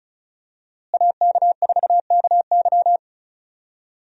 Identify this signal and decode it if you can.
Morse code, heard as a keyed tone, about 36 wpm, 700 Hz. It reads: AK4KY
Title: USA Call Signs - 5 Characters - Repeated 3x - 35wpm